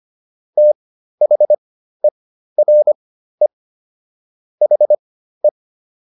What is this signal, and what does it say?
Morse code, heard as a keyed tone, about 25 wpm, spaced as for 13 wpm, 600 Hz. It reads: THERE HE